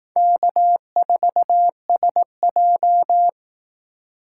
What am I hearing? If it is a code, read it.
Morse code, heard as a keyed tone, about 18 wpm, 700 Hz. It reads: K4SJ
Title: Speed Racing - USA 1x2 Call Signs 15wpm